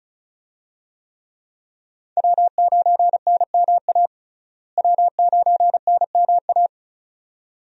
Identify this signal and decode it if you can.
Morse code, heard as a keyed tone, about 36 wpm, 700 Hz. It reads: W9NMA W9NMA
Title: USA Call Signs - 5 Characters - Repeated 2x - 35wpm